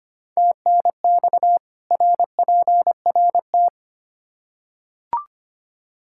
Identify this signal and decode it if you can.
Morse code, heard as a keyed tone, about 25 wpm, 700 Hz. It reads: TNX RPRT
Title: Practice QSO - Exchange 2 - Info - 25wpm